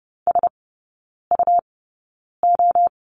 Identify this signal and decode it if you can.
Morse code, heard as a keyed tone, about 30 wpm, spaced as for 9 wpm, 700 Hz. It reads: SUO
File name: Lesson 17e - Letter u - Three Characters 30wpm